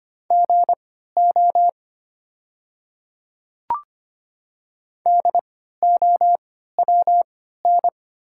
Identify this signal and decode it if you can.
Morse code, heard as a keyed tone, about 25 wpm, spaced as for 14 wpm, 700 Hz. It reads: GO DOWN
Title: 3x Character Spacing - Top 100 Words - 25wpm